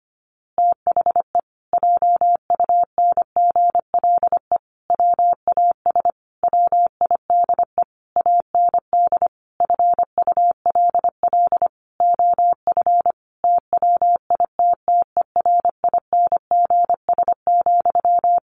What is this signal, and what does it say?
Morse code, heard as a keyed tone, about 25 wpm, 700 Hz. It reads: THE JUNGLE WAS WIDE AND FULL OF TWITTERINGS,
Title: A Sound Of Thunder 25wpm